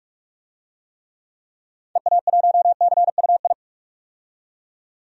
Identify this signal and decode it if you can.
Morse code, heard as a keyed tone, about 42 wpm, 700 Hz. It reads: EA1CUI
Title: DXCC-100 Call Signs - 6 Characters - 45wpm